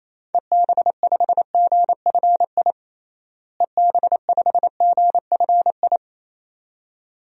Morse code, heard as a keyed tone, about 28 wpm, 700 Hz. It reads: EB5GFI EB5GFI
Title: DXCC-100 Call Signs - 6 Characters - Repeated 2x - 28wpm